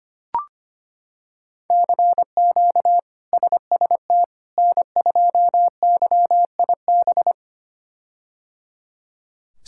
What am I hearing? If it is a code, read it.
Morse code, heard as a keyed tone, about 25 wpm, 700 Hz. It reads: CQ SST N2YIB
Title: K1USN SST QSO Practice - 25wpm